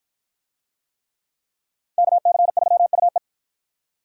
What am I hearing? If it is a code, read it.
Morse code, heard as a keyed tone, about 53 wpm, 700 Hz. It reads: KC3UE